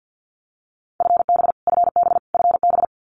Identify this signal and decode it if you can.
Morse code, heard as a keyed tone, about 50 wpm, 700 Hz. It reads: FB FB FB
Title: Abbreviations Repeated 3x - 50wpm